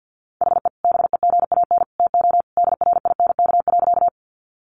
Morse code, heard as a keyed tone, about 50 wpm, 700 Hz. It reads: HE BEGAN TO DRINK.
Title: Sentences from Top 1500 Words 50wpm